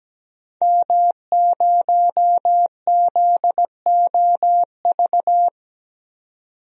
Morse code, heard as a keyed tone, about 17 wpm, 700 Hz. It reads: M0ZOV
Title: DXCC-100 Call Signs - 5 Characters - 17wpm